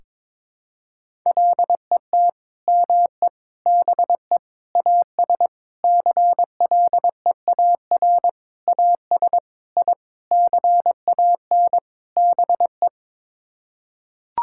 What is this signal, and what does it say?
Morse code, heard as a keyed tone, about 22 wpm, 700 Hz. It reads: LET ME BE AS CLEAR AS I CAN BE
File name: Sets of 9 Words - 22wpm